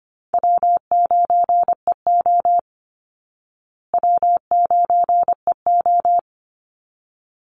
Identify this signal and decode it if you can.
Morse code, heard as a keyed tone, about 25 wpm, 700 Hz. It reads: W9EO W9EO
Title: Call Signs Repeated 3x - 25wpm